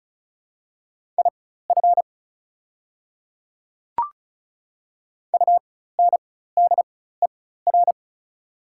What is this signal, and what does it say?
Morse code, heard as a keyed tone, about 36 wpm, spaced as for 16 wpm, 700 Hz. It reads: IF UNDER
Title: 4x Character Spacing - Top 500 Words - 35wpm